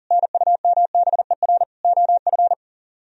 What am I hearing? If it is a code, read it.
Morse code, heard as a keyed tone, about 52 wpm, 700 Hz. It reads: NUMBER OF